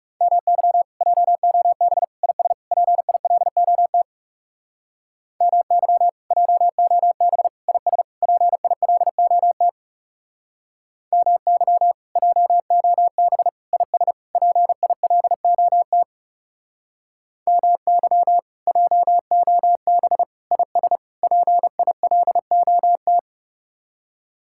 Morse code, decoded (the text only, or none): MY JOB IS PILOT MY JOB IS PILOT MY JOB IS PILOT MY JOB IS PILOT